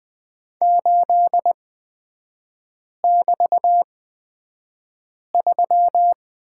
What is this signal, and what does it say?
Morse code, heard as a keyed tone, about 20 wpm, spaced as for 5 wpm, 690 Hz. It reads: 8<BT>3
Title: 236 Lesson 38e - Letter q - Three Characters 20wpm